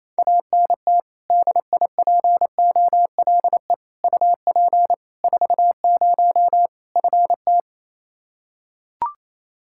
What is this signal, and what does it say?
Morse code, heard as a keyed tone, about 27 wpm, 700 Hz. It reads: ANT DIPOLE UP 40 FT
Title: Practice QSO - Exchange 2 - Info - 28wpm